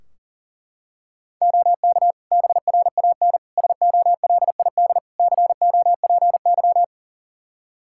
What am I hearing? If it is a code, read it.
Morse code, heard as a keyed tone, about 40 wpm, 700 Hz. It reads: OK BRAN SOLID COPY
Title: Practice QSO - Exchange 3 - More Info - 40wpm